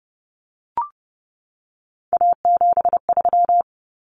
Morse code, heard as a keyed tone, about 30 wpm, 700 Hz. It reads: A73
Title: Common Cut Numbers - 3-Digit 30wpm